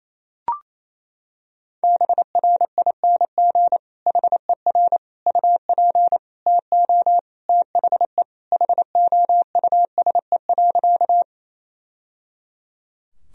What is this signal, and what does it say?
Morse code, heard as a keyed tone, about 28 wpm, 700 Hz. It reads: BRING HER UP TO THE HOUSE.
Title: Sentences from Top 1000 Words - Review Part 3 - 28wpm